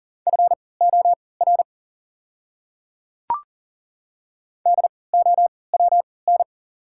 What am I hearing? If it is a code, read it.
Morse code, heard as a keyed tone, about 40 wpm, spaced as for 23 wpm, 700 Hz. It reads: FOR DOWN
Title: 3x Character Spacing - Top 100 Words - 40wpm